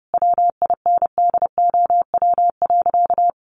Morse code, heard as a keyed tone, about 30 wpm, 700 Hz. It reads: WINDOW.
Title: Additional Easy Phrases 30wpm